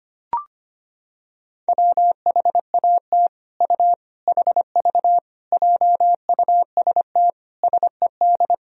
Morse code, heard as a keyed tone, about 25 wpm, 700 Hz. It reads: WHAT U HV JUST SED
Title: Sentences from Top 200 Words and Abbreviations 25wpm